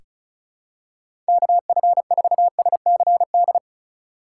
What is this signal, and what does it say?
Morse code, heard as a keyed tone, about 36 wpm, 700 Hz. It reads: KF4SCD
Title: USA Call Signs - 4x Character Spacing - 35wpm